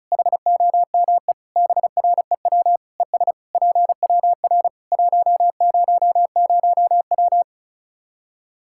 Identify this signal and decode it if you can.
Morse code, heard as a keyed tone, about 34 wpm, 700 Hz. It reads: HOME BREW ES PWR 100W
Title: Practice QSO - Exchange 2 - Info - 35wpm